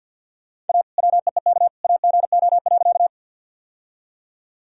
Morse code, heard as a keyed tone, about 50 wpm, 700 Hz. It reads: A WEEK AGO.
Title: Sentences from Top 1000 Words - Review Part 3 - 50wpm